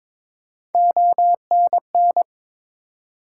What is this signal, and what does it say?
Morse code, heard as a keyed tone, about 22 wpm, 700 Hz. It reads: ONN
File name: ARRL Sections v2 - 22wpm